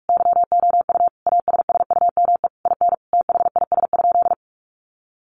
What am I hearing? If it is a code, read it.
Morse code, heard as a keyed tone, about 44 wpm, 700 Hz. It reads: YOU ASSUME IN THIS?
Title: Sentences from Top 1500 Words 45wpm